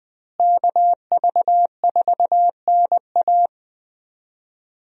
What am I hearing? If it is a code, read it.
Morse code, heard as a keyed tone, about 20 wpm, 700 Hz. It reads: KV4NA
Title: USA Call Signs - 5 Characters - Repeated 2x - 20wpm